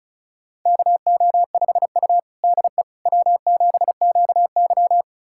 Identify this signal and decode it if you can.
Morse code, heard as a keyed tone, about 36 wpm, 700 Hz. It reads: KO5U DE W7QY
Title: USA Callsign DE Callsign - 4 Characters - 35wpm